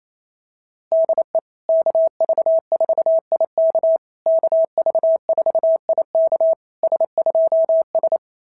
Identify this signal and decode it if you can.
Morse code, heard as a keyed tone, about 29 wpm, 660 Hz. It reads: DE KV4IK KV4IK S2S
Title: SOTA QSO Practice 28wpm